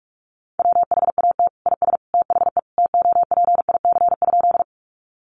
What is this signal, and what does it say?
Morse code, heard as a keyed tone, about 44 wpm, 700 Hz. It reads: WHAT IS THE TOPIC?
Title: Sentences from Top 1200 Words 45wpm